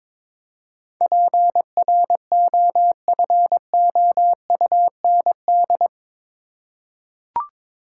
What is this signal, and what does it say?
Morse code, heard as a keyed tone, about 22 wpm, 700 Hz. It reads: PROFOUND
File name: practice / 22wpm / Eight Letter Words - 22wpm